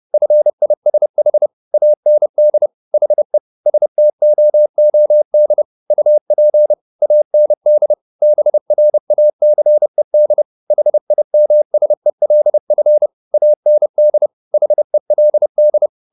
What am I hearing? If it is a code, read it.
Morse code, heard as a keyed tone, about 30 wpm, 600 Hz. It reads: FISH AND HE STOOD UP AND BRACED HIMSELF AND HELD
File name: the old man and the sea / the old man and the sea 1 30wpm